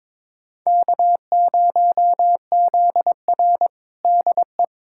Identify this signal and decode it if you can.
Morse code, heard as a keyed tone, about 22 wpm, 700 Hz. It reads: K0ZR DE
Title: Practice QSO - Exchange 4 - Ending - 22wpm